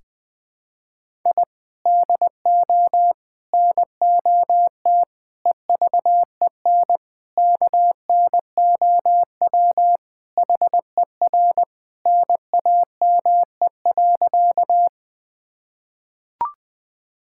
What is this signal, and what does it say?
Morse code, heard as a keyed tone, about 20 wpm, 700 Hz. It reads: I DO NOT EVEN KNOW HER NAME.
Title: Sentences with 5-Word Phrases - 20wpm